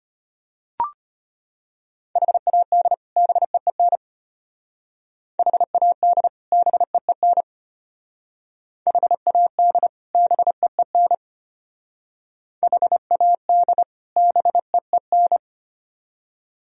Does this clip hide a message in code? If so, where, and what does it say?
Yes: Morse code from 2.2 to 15.4 s, heard as a keyed tone, about 32 wpm, 700 Hz. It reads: HAD BEEN HAD BEEN HAD BEEN HAD BEEN